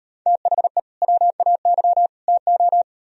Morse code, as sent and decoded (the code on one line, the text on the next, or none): - .... / .-- .- -.-- / - ---
THE WAY TO